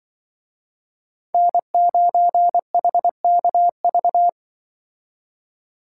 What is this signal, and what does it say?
Morse code, heard as a keyed tone, about 25 wpm, 700 Hz. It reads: N9HKV
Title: Speed Racing - USA 1x3 Call Signs 20wpm